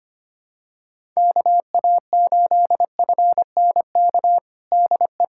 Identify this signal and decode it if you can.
Morse code, heard as a keyed tone, about 25 wpm, 700 Hz. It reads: KA8FNK DE
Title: USA Callsign DE Callsign - 6 Characters - 25wpm